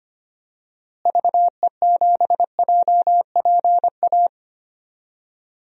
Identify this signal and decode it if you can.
Morse code, heard as a keyed tone, about 25 wpm, 700 Hz. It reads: VE7JPA
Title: DXCC-100 Call Signs - 6 Characters - Repeated 2x - 25wpm